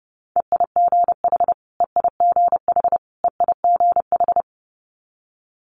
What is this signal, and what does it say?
Morse code, heard as a keyed tone, about 30 wpm, 700 Hz. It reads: EIGH EIGH EIGH
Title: Multi-letter Phonemes Repeated 3x 30wpm